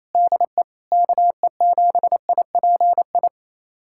Morse code, heard as a keyed tone, about 28 wpm, 700 Hz. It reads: DE KE7IPI
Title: USA Callsign DE Callsign - 6 Characters - 28wpm